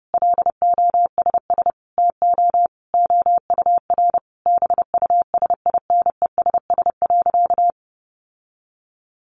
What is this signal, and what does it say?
Morse code, heard as a keyed tone, about 30 wpm, 700 Hz. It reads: LOSS TO OUR BUSINESS.
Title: Sentences from Top 1400 Words 30wpm